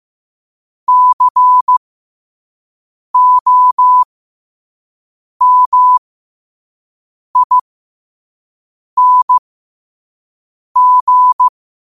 Morse code, heard as a keyed tone, about 15 wpm, spaced as for 5 wpm, 1000 Hz. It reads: COMING
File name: qotdfiles / 5wpm